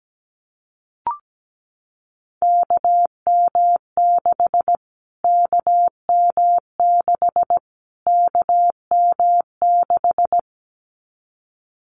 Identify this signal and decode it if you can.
Morse code, heard as a keyed tone, about 17 wpm, 700 Hz. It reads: KM6 KM6 KM6
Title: Callsign Prefixes for Top 25 DXCC - Repeated 3x - 17wpm